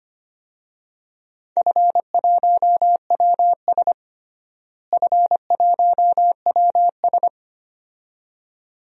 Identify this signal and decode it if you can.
Morse code, heard as a keyed tone, about 25 wpm, 700 Hz. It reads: F1WS F1WS
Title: DXCC-100 Call Signs - 4 Characters - Repeated 2x - 25wpm